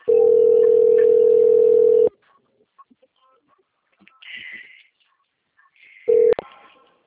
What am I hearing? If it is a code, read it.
Ringback tone (North America), 440 + 480 Hz.
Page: Aracatuba - SP